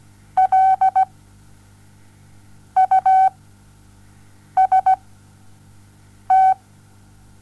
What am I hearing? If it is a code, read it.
Morse code, heard as a keyed tone, about 17 wpm, spaced as for 5 wpm, 760 Hz. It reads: LUST
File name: Onlinelehrgang / morsen / DJ4UF Morsen07